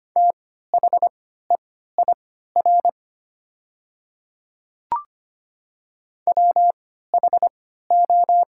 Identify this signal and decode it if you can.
Morse code, heard as a keyed tone, about 25 wpm, spaced as for 14 wpm, 700 Hz. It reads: THEIR WHO
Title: 3x Character Spacing - Top 100 Words - 25wpm